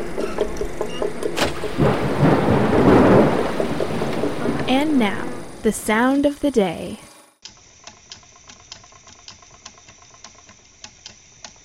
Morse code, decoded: none

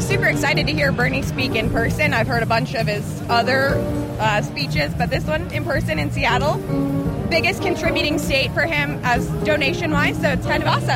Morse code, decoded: none

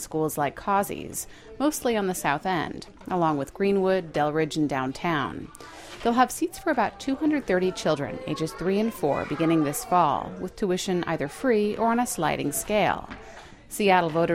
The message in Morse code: none

